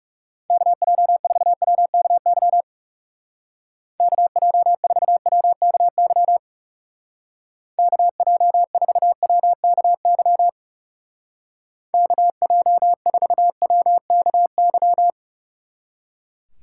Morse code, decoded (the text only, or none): KJ4WKY KJ4WKY KJ4WKY KJ4WKY